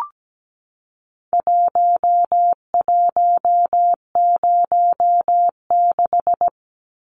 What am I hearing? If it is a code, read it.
Morse code, heard as a keyed tone, about 17 wpm, 700 Hz. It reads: 1106
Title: Numbers - Contest Style - 4-Digit 17wpm